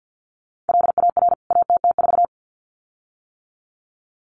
Morse code, heard as a keyed tone, about 50 wpm, 700 Hz. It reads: LAR ATT4
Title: CWT Exchange 50wpm